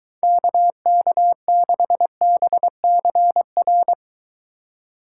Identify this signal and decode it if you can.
Morse code, heard as a keyed tone, about 23 wpm, 700 Hz. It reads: KK6BCR